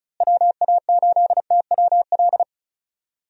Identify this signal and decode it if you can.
Morse code, heard as a keyed tone, about 36 wpm, 700 Hz. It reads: WA8TWL
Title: USA Call Signs - 6 Characters - Repeated 3x - 35wpm